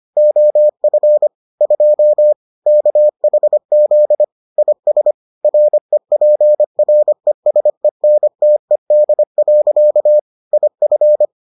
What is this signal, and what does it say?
Morse code, heard as a keyed tone, about 25 wpm, 600 Hz. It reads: OF 2 KHZ IS REPRESENTED. IF